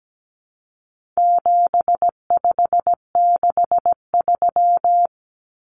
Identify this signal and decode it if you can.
Morse code, heard as a keyed tone, about 17 wpm, 700 Hz. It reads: 7563
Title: Numbers - 4-Digit 17wpm